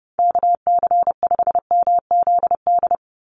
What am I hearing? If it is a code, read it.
Morse code, heard as a keyed tone, about 30 wpm, 700 Hz. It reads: KC5MZD
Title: USA Call Signs 30wpm